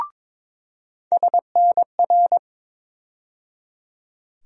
Morse code, heard as a keyed tone, about 22 wpm, 700 Hz. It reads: SNR
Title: USA Call Sign Postfixes - 3 Letters 22wpm